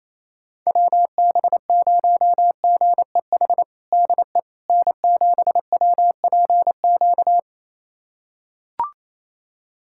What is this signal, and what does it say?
Morse code, heard as a keyed tone, about 28 wpm, 700 Hz. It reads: WB0GEH DE N7WPQ
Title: USA Callsign DE Callsign 28wpm